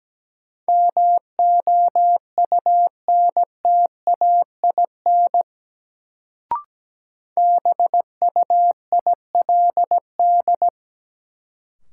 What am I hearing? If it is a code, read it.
Morse code, heard as a keyed tone, about 17 wpm, 700 Hz. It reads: MOUNTAIN BUILD
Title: Sentences from Top 1000 Words - Review Part 1 - New Word Drill - 17wpm